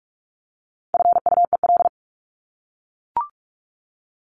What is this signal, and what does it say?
Morse code, heard as a keyed tone, about 44 wpm, 700 Hz. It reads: FUEL
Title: Top 1751-2000 Words 45wpm